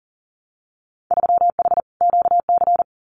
Morse code, heard as a keyed tone, about 40 wpm, 700 Hz. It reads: 3H QC
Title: Winter Field Day Exchanges 40wpm